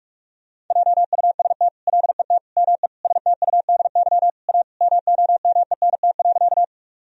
Morse code, heard as a keyed tone, about 41 wpm, 700 Hz. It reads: JUST LET ME STUDY A MOMENT.
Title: Sentences from Top 1000 Words - Review Part 1 - 45wpm